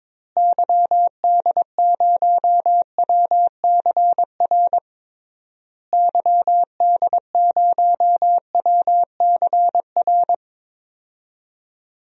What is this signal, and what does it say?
Morse code, heard as a keyed tone, about 21 wpm, 700 Hz. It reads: YD0WCR YD0WCR
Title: DXCC-100 Call Signs - 6 Characters - Repeated 3x - 22wpm